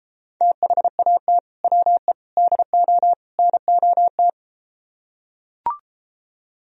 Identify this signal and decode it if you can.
Morse code, heard as a keyed tone, about 33 wpm, 700 Hz. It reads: THAT WE DO NOT